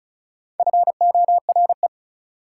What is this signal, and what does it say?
Morse code, heard as a keyed tone, about 36 wpm, 700 Hz. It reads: FORE
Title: Prefixes and Suffixes 35wpm